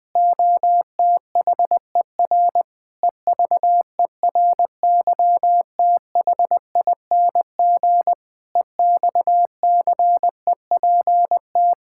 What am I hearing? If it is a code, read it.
Morse code, heard as a keyed tone, about 20 wpm, 700 Hz. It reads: OTHER EVERYTHING EXCEPT
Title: Sentences from Top 1600 Words 20wpm